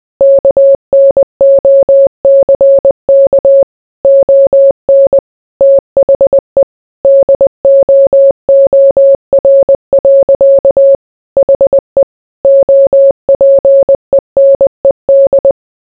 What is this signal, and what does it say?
Morse code, heard as a keyed tone, about 20 wpm, 550 Hz. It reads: KNOCK ON THE DOOR. HE OPENED